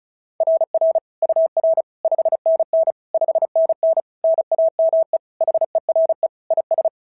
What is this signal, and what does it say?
Morse code, heard as a keyed tone, about 36 wpm, 690 Hz. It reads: RR UR 5NN 5NN NAME HERE IS
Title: SOTA QSO Practice 35wpm